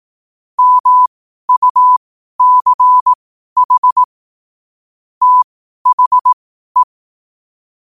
Morse code, heard as a keyed tone, about 18 wpm, spaced as for 12 wpm, 1000 Hz. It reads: MUCH THE
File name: qotdfiles / 12wpm